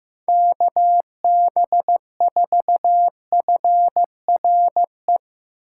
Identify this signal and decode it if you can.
Morse code, heard as a keyed tone, about 15 wpm, 700 Hz. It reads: KB4FRE
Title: USA Call Signs - 6 Characters - 3x Character Spacing - 15wpm